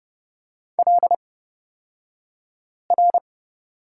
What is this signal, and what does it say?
Morse code, heard as a keyed tone, about 30 wpm, spaced as for 5 wpm, 700 Hz. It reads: LR
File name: Lesson 36d - BT - Two Characters 30wpm